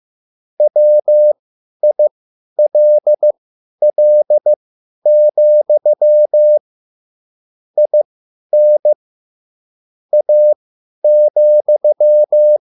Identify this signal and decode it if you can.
Morse code, heard as a keyed tone, about 15 wpm, spaced as for 10 wpm, 600 Hz. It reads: WILL, IN A,